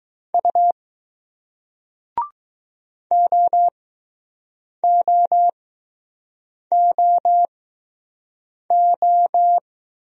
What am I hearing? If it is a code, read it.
Morse code, heard as a keyed tone, about 20 wpm, spaced as for 6 wpm, 700 Hz. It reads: U OOOO